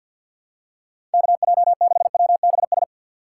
Morse code, heard as a keyed tone, about 50 wpm, 700 Hz. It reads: KJ6WBS
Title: USA Call Signs - 6 Characters - 3x Character Spacing - 50wpm